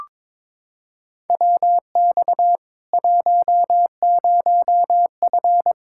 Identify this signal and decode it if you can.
Morse code, heard as a keyed tone, about 22 wpm, 700 Hz. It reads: WX 10F
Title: Practice QSO - Exchange 2 - Info - 22wpm